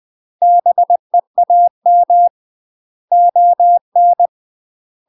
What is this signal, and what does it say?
Morse code, heard as a keyed tone, about 20 wpm, 700 Hz. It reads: BEAM ON
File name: UK MASTER QSOs TRAINING / 20WPM / UK MASTER QSOs 0032